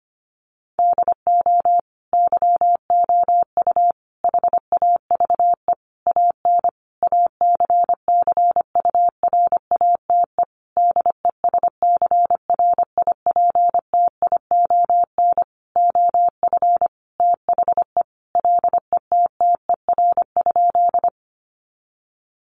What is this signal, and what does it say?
Morse code, heard as a keyed tone, about 25 wpm, 700 Hz. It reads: DO YOU HAVE AN ACCURATE DESCRIPTION OF THE LETTER?
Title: Sentences from Top 1300 Words 25wpm